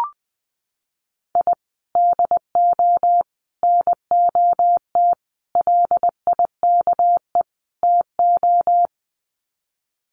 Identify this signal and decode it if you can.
Morse code, heard as a keyed tone, about 20 wpm, 700 Hz. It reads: I DO NOT LIKE TO
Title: Sets of 5 Words 20wpm